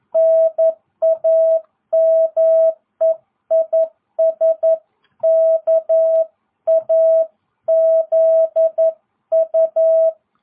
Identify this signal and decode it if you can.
Morse code, heard as a keyed tone, about 11 wpm, 650 Hz. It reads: NAME IS KAZU